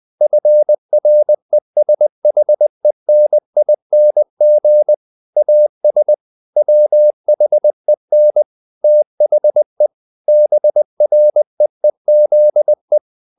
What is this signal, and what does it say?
Morse code, heard as a keyed tone, about 20 wpm, 600 Hz. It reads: FRESHENING AS WHEN THE BREEZE
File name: the old man and the sea / the old man and the sea part 1 20wpm